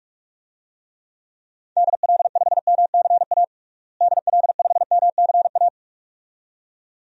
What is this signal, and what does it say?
Morse code, heard as a keyed tone, about 47 wpm, 700 Hz. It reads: DL5MCA DL5MCA
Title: DXCC-100 Call Signs - 6 Characters - Repeated 2x - 45wpm